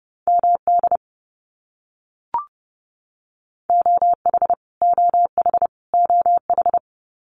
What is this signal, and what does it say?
Morse code, heard as a keyed tone, about 30 wpm, 700 Hz. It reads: MD OH OH OH
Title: US States Repeated 3x 30wpm